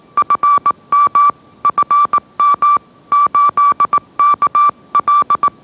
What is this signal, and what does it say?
Morse code, heard as a keyed tone, about 18 wpm, 1160 Hz. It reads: FM FM 8KL
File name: mrx / mrx082 20wpm